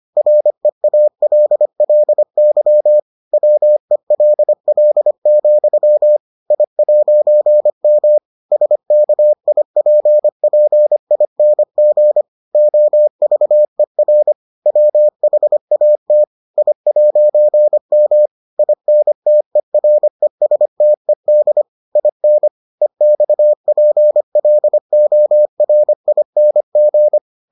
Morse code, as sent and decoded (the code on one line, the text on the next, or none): .-. .- .-.. .-.. -.-- / .-- . .-.. .-.. --..-- / .. .----. -- / ... -.- .. .--. .--. .. -. --. / --- ...- . .-. / .-- .... .- - / .. .----. -- / .. -. - . .-. ... - . -.. / .. -. / . -..- .--. .-.. --- .-. .. -. --.
REALLY WELL, I'M SKIPPING OVER WHAT I'M INTERESTED IN EXPLORING